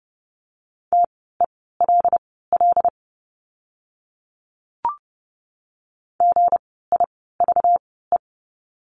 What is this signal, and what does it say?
Morse code, heard as a keyed tone, about 30 wpm, spaced as for 17 wpm, 700 Hz. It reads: TELL GIVE
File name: course / 30:10wpm / 237 Lesson 38f - Letter q - Words 30wpm